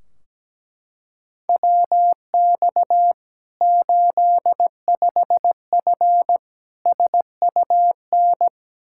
Morse code, heard as a keyed tone, about 17 wpm, 700 Hz. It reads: WX 85F SUN
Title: Practice QSO - Exchange 2 - Info - 17wpm